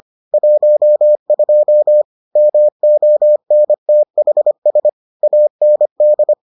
Morse code, heard as a keyed tone, about 25 wpm, 600 Hz. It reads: 12 MONTHS AND